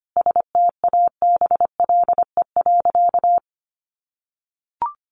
Morse code, heard as a keyed tone, about 25 wpm, 700 Hz. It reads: STABLE.
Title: Sentences from Top 1500 Words 25wpm